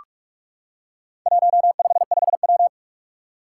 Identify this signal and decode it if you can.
Morse code, heard as a keyed tone, about 44 wpm, 700 Hz. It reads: 155W
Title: Four Challenging Letter-Number Combinations 45wpm